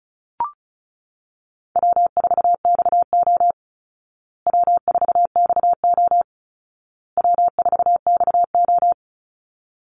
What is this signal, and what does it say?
Morse code, heard as a keyed tone, about 34 wpm, 700 Hz. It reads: W4XO W4XO W4XO
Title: Easy Call Signs Repeated 3x - 35wpm